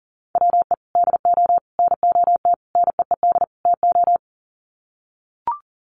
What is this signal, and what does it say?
Morse code, heard as a keyed tone, about 40 wpm, 700 Hz. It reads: WE DO NOT NEED TO
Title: Sets of 5 Words Encore 40wpm